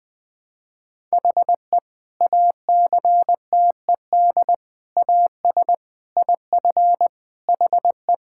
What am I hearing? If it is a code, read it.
Morse code, heard as a keyed tone, about 20 wpm, 700 Hz. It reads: HE ACTED AS IF HE
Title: Sentences with 4-Word Phrases - 20wpm